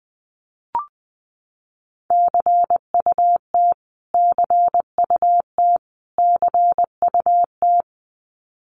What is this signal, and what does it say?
Morse code, heard as a keyed tone, about 20 wpm, 700 Hz. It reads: CUT CUT CUT
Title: Top 500 Words Repeated 20wpm